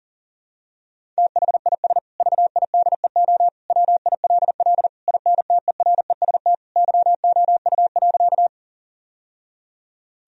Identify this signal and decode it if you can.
Morse code, heard as a keyed tone, about 37 wpm, 700 Hz. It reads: THIS VIDEO WILL INTEREST YOU.
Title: Sentences from Top 1000 Words - Review Part 4 - 40wpm